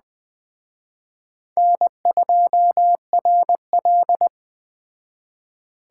Morse code, heard as a keyed tone, about 20 wpm, 700 Hz. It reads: N2RL
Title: Speed Racing - USA 1x2 Call Signs 17wpm